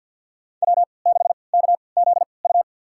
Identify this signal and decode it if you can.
Morse code, heard as a keyed tone, about 52 wpm, spaced as for 29 wpm, 700 Hz. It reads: W6KZU
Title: USA Call Signs - 5 Characters - 3x Character Spacing - 50wpm